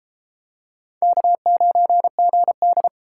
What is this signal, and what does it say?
Morse code, heard as a keyed tone, about 33 wpm, 700 Hz. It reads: K9GD